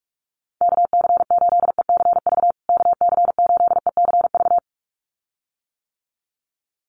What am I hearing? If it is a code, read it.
Morse code, heard as a keyed tone, about 44 wpm, 700 Hz. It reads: KC8ECV KC8ECV
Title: USA Call Signs Repeated 2x 45wpm